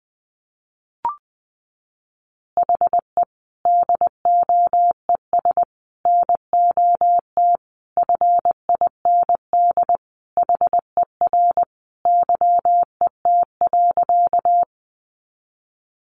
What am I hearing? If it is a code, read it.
Morse code, heard as a keyed tone, about 20 wpm, 700 Hz. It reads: HE DOES NOT FIND HER YET.
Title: Easy Phrases 20wpm